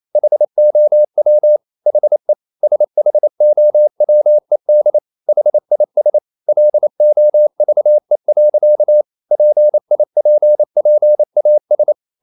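Morse code, heard as a keyed tone, about 28 wpm, 600 Hz. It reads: HOW HE SHOWED HIS LOVE. PIPPAS
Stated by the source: Morse code practice at 28 wpm, text of Ken Follett's novel Never in English, from CW med Gnister.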